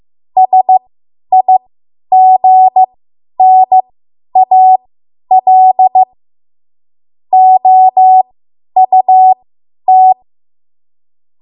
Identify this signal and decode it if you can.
Morse code, heard as a keyed tone, about 15 wpm, spaced as for 10 wpm, 750 Hz. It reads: SIGNAL OUT
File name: Archive / 10 WPM / 10WPM